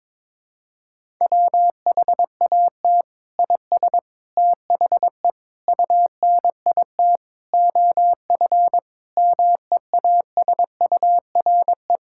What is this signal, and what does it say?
Morse code, heard as a keyed tone, about 22 wpm, 700 Hz. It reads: WHAT IS THE UNIT OF MEASURE
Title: Sentences from Top 1000 Words - Review Part 6 - 22wpm